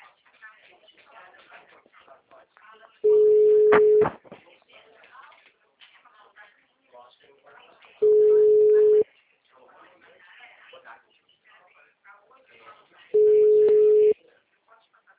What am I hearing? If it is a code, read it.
Ringback tone (Europe, 425 Hz), with speech.
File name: out- -2004- - - .95159